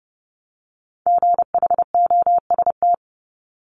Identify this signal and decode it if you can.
Morse code, heard as a keyed tone, about 30 wpm, 700 Hz. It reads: GHOST
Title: Multi-letter Phonemes Repeated 3x 30wpm